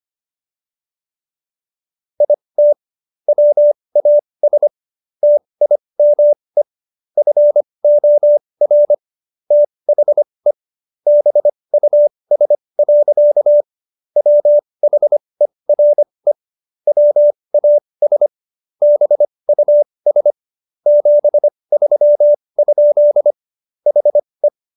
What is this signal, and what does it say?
Morse code, heard as a keyed tone, about 25 wpm, spaced as for 20 wpm, 590 Hz. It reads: IT WAS TIME FOR THE BUS. WHERE WAS BUS 73? HE